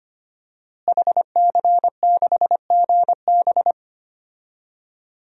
Morse code, heard as a keyed tone, about 25 wpm, 700 Hz. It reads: HC6GB
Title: DXCC-100 Call Signs - 5 Characters - Repeated 2x - 25wpm